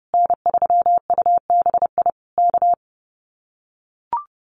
Morse code, heard as a keyed tone, about 30 wpm, 700 Hz. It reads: N3UBI K
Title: Common QSO Phrases Encore 30wpm